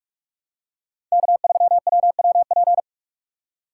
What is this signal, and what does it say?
Morse code, heard as a keyed tone, about 46 wpm, 700 Hz. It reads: K3WWP
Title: K1USN SST QSO Practice - 45wpm